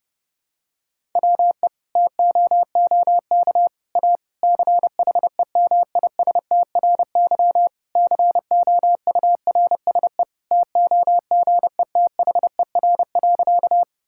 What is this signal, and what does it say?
Morse code, heard as a keyed tone, about 32 wpm, 700 Hz. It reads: WE TOOK A CHEMISTRY COURSE TOGETHER.
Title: Sentences from Top 1000 Words - Review Part 6 - 30wpm